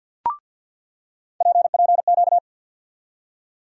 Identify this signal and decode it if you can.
Morse code, heard as a keyed tone, about 49 wpm, 700 Hz. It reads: PPQ
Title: USA Call Sign Postfixes - 3 Letters 50wpm